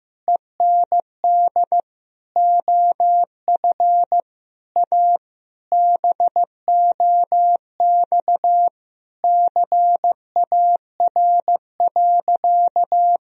Morse code, heard as a keyed tone, about 15 wpm, 700 Hz. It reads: END OF A BOX CAR.
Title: Sentences from Top 1000 Words - Review Part 2 - 15wpm